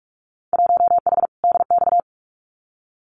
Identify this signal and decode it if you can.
Morse code, heard as a keyed tone, about 46 wpm, 700 Hz. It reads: 1H DX
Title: Winter Field Day Exchanges 45wpm